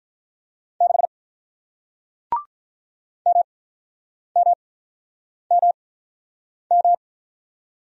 Morse code, heard as a keyed tone, about 48 wpm, spaced as for 9 wpm, 700 Hz. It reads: 6 MMMM